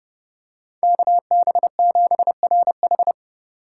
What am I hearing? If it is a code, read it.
Morse code, heard as a keyed tone, about 30 wpm, 700 Hz. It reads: KB7RH
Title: USA Call Signs - 3x Character Spacing - 30wpm